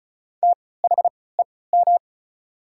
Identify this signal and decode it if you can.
Morse code, heard as a keyed tone, about 36 wpm, spaced as for 20 wpm, 690 Hz. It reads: THEM
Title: 3x Character Spacing - Top 300 Words - 35wpm